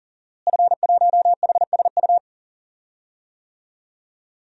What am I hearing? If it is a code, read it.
Morse code, heard as a keyed tone, about 40 wpm, 700 Hz. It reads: F1HSU
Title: DXCC-100 Call Signs - 5 Characters - Repeated 2x - 40wpm